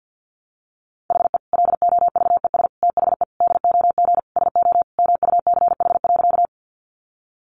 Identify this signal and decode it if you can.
Morse code, heard as a keyed tone, about 50 wpm, 700 Hz. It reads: HE LOVES THE DOG SO MUCH.
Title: Additional Easy Phrases 50wpm